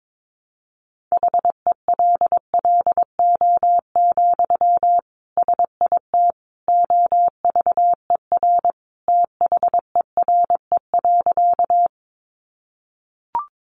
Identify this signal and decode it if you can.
Morse code, heard as a keyed tone, about 22 wpm, 700 Hz. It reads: HELLO, SIT OVER THERE.
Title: Sentences from Top 1100 Words 22wpm